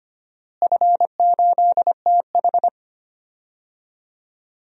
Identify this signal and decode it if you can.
Morse code, heard as a keyed tone, about 25 wpm, 700 Hz. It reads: F8TH
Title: DXCC-100 Call Signs - 4 Characters - Repeated 2x - 25wpm